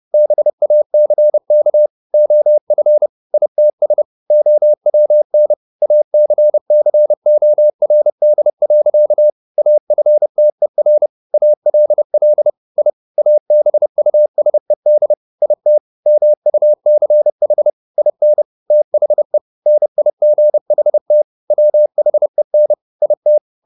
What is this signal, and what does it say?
Morse code, heard as a keyed tone, about 30 wpm, 600 Hz. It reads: BACK OF ITS OWN ACCORD. AFTER ALL I ABUSED IT MUCH IN THE NIGHT WHEN IT
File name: the old man and the sea / the old man and the sea 1 30wpm